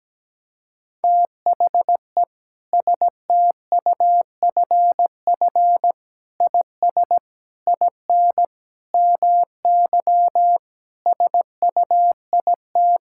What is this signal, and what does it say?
Morse code, heard as a keyed tone, about 17 wpm, 700 Hz. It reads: THE STUFF IS IN MY SUIT
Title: Sentences from Top 1600 Words 17wpm